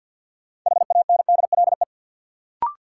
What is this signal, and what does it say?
Morse code, heard as a keyed tone, about 52 wpm, 700 Hz. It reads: HANDLE
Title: Sentences from Top 1100 Words - New Word Drill - 50wpm